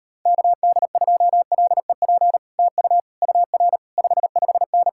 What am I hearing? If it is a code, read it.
Morse code, heard as a keyed tone, about 33 wpm, 700 Hz. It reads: KD2LEP TU UR 55N